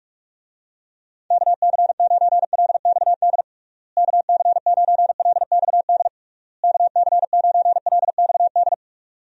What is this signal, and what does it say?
Morse code, heard as a keyed tone, about 47 wpm, 700 Hz. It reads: KC9LXD KC9LXD KC9LXD
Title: USA Call Signs - 6 Characters - Repeated 3x - 45wpm